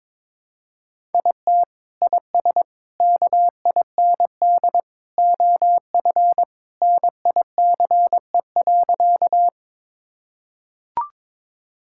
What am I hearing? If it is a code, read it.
Morse code, heard as a keyed tone, about 22 wpm, 700 Hz. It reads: IT IS KIND OF NICE.
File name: practice / 22wpm / Sentences with 2-Word Phrases - 22wpm